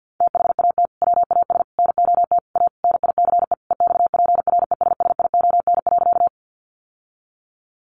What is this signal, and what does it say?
Morse code, heard as a keyed tone, about 50 wpm, 700 Hz. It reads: THAT WAS NOT A NICE EXPRESSION.
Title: Sentences from Top 1200 Words 50wpm